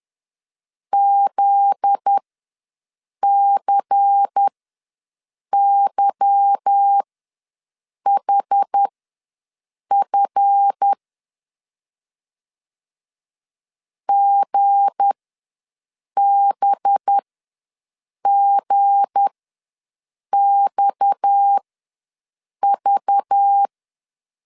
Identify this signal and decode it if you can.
Morse code, heard as a keyed tone, about 10 wpm, spaced as for 6 wpm, 800 Hz. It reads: ZCYHF GBGXV
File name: LEZIONI / 8 YJ45 / 8 Y ws 10wpm 3952